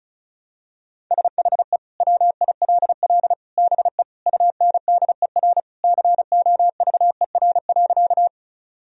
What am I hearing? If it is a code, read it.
Morse code, heard as a keyed tone, about 34 wpm, 700 Hz. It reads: SHE WILL BE UNDER COVER.
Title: Sentences from Top 1000 Words - Review Part 1 - 35wpm